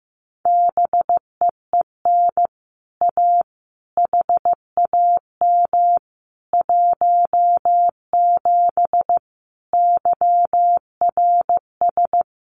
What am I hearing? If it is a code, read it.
Morse code, heard as a keyed tone, about 15 wpm, 700 Hz. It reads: BEEN A HAM 17 YRS
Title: Common QSO Phrases Encore 15wpm